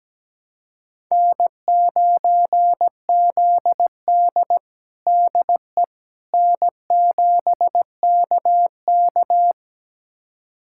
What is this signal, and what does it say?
Morse code, heard as a keyed tone, about 17 wpm, 700 Hz. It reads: N9ZD DE N7KK
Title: USA Callsign DE Callsign - 4 Characters - 17wpm